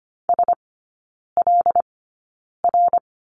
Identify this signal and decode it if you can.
Morse code, heard as a keyed tone, about 25 wpm, spaced as for 9 wpm, 700 Hz. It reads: SLR
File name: Lesson 14e - Number 2 - Three Characters 25wpm